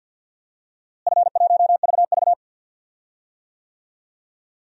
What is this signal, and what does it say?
Morse code, heard as a keyed tone, about 53 wpm, 700 Hz. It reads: F1VV
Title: DXCC-100 Call Signs - 4 Characters - Repeated 2x - 50wpm